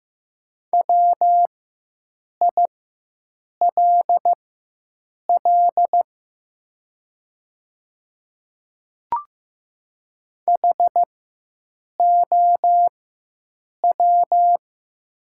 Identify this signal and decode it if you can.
Morse code, heard as a keyed tone, about 15 wpm, spaced as for 7 wpm, 700 Hz. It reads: WILL HOW
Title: 4x Character Spacing - Top 100 Words - 15wpm